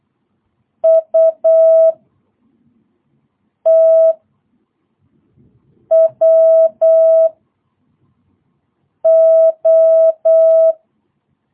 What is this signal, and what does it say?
Morse code, heard as a keyed tone, about 8 wpm, 650 Hz. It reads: U T W O
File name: jh8cbh / 2022cw